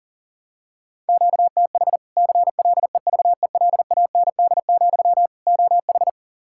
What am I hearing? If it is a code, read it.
Morse code, heard as a keyed tone, about 40 wpm, 700 Hz. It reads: QTH CLEVELAND, OH
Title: Practice QSO - Exchange 1 - Beginning - 40wpm